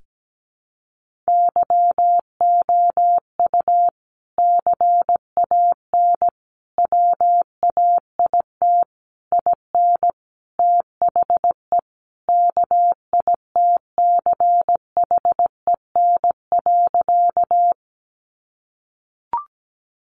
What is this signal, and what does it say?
Morse code, heard as a keyed tone, about 17 wpm, 700 Hz. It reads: YOU CAN WAIT IN THE KITCHEN.
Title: Sentences from Top 1400 Words 17wpm